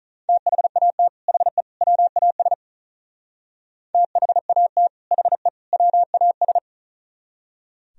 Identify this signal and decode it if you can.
Morse code, heard as a keyed tone, about 43 wpm, 700 Hz. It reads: THAT HE WAS THAT HE WAS
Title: Speed-Racing - Sets of 3 Words 35wpm